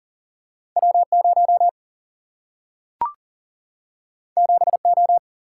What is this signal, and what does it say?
Morse code, heard as a keyed tone, about 40 wpm, 700 Hz. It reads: W0 7O
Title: Single-Letter Single-Number 40wpm